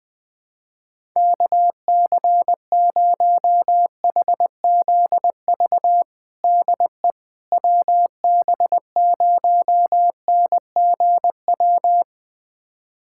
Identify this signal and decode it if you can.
Morse code, heard as a keyed tone, about 20 wpm, 700 Hz. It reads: KC0HZV DE WB0NGW
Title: USA Callsign DE Callsign - 6 Characters - 20wpm